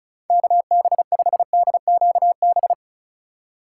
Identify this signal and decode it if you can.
Morse code, heard as a keyed tone, about 42 wpm, 700 Hz. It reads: KB5DQB
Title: Speed Racing - USA Call Signs 35wpm